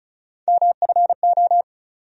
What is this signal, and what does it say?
Morse code, heard as a keyed tone, about 35 wpm, 700 Hz. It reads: MFO
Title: USA Call Sign Postfixes - 3 Letters 35wpm